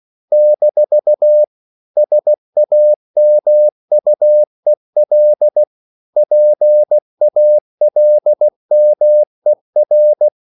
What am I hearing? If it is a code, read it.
Morse code, heard as a keyed tone, about 16 wpm, 600 Hz. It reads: - SAMUEL PALMER
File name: CW / Practice / EnglishQuotations0216